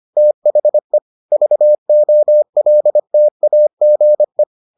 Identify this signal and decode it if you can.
Morse code, heard as a keyed tone, about 25 wpm, 600 Hz. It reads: THE VOLTAGE